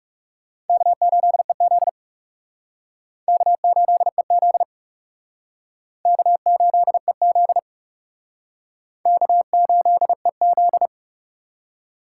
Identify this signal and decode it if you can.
Morse code, heard as a keyed tone, about 37 wpm, 700 Hz. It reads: K8EZ K8EZ K8EZ K8EZ
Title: Speed Racing - USA Call Signs 30wpm